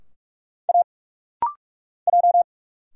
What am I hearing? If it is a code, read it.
Morse code, heard as a keyed tone, about 46 wpm, 700 Hz. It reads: A J
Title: Single Letters - Rapid-Fire 45wpm